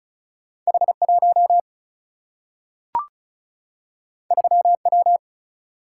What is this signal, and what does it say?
Morse code, heard as a keyed tone, about 36 wpm, 700 Hz. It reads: H 1 3 W
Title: Single-Letter Single-Number 35wpm